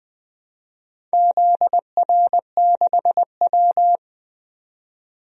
Morse code, heard as a keyed tone, about 20 wpm, 700 Hz. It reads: ZR6W
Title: DXCC-100 Call Signs - 4 Characters - 20wpm